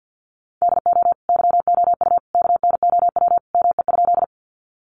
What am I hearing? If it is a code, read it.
Morse code, heard as a keyed tone, about 50 wpm, 700 Hz. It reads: DO YOU KNOW ME?
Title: Sentences from Top 200 Words 50wpm